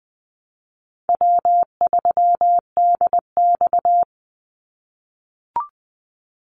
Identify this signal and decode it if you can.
Morse code, heard as a keyed tone, about 20 wpm, 700 Hz. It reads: W3DX
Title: NA SOTA Activator Callsigns 20wpm